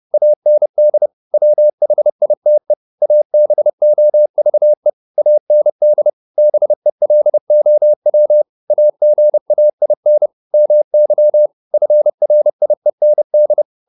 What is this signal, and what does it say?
Morse code, heard as a keyed tone, about 30 wpm, 600 Hz. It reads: AND WHITE ABOVE AND BELOW AGAIN MY FRIEND